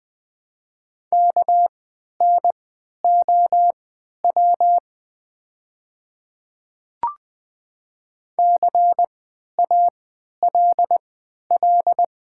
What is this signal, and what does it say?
Morse code, heard as a keyed tone, about 20 wpm, spaced as for 11 wpm, 700 Hz. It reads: KNOW CALL
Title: 3x Character Spacing - Top 100 Words - 20wpm